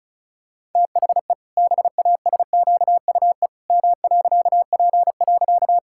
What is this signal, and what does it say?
Morse code, heard as a keyed tone, about 34 wpm, 700 Hz. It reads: THE BASQUE M.P.
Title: Pangram Challenge v2 - 35wpm